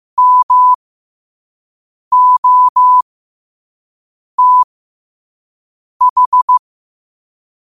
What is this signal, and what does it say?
Morse code, heard as a keyed tone, about 15 wpm, spaced as for 5 wpm, 1000 Hz. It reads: MOTH
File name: qotdfiles / 5wpm